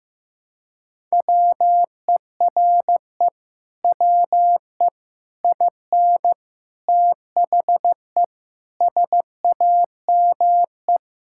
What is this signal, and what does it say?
Morse code, heard as a keyed tone, about 15 wpm, 700 Hz. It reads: WERE WE IN THE SAME
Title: Sentences from Top 1000 Words - Review Part 6 - 15wpm